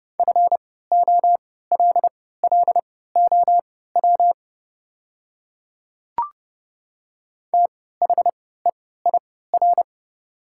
Morse code, heard as a keyed tone, about 30 wpm, spaced as for 17 wpm, 700 Hz. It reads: FOLLOW THEIR
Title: 3x Character Spacing - Top 300 Words - 30wpm